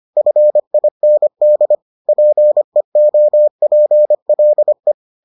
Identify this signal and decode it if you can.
Morse code, heard as a keyed tone, about 25 wpm, 600 Hz. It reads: FIND PEOPLE